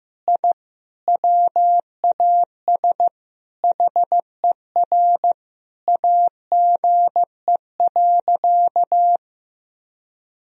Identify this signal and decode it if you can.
Morse code, heard as a keyed tone, about 15 wpm, 700 Hz. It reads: I WAS HER AGE.
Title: Sentences from Top 1000 Words - Review Part 3 - 15wpm